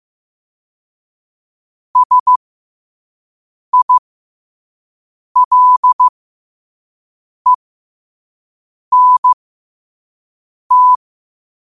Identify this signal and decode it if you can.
Morse code, heard as a keyed tone, about 15 wpm, spaced as for 5 wpm, 1000 Hz. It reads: SILENT
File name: qotdfiles / 5wpm